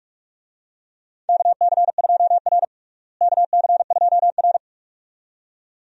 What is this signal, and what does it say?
Morse code, heard as a keyed tone, about 46 wpm, 700 Hz. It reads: KC2R KC2R
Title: USA Call Signs - 4 Characters - Repeated 2x - 45wpm